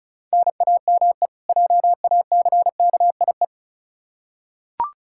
Morse code, heard as a keyed tone, about 34 wpm, 700 Hz. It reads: NAME JACKIE
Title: K1USN SST QSO Practice 35wpm